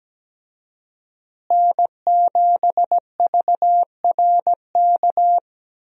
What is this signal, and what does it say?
Morse code, heard as a keyed tone, about 17 wpm, 700 Hz. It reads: N7VRK
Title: USA Call Signs - 5 Characters - Repeated 2x - 17wpm